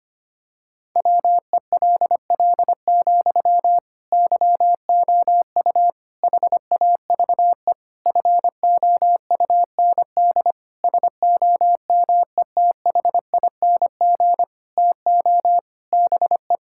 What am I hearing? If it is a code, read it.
Morse code, heard as a keyed tone, about 25 wpm, 700 Hz. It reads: WELL, YOU HAVE FOUND SOMETHING TO BE
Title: Sentences from Top 1000 Words - Review Part 6 - 25wpm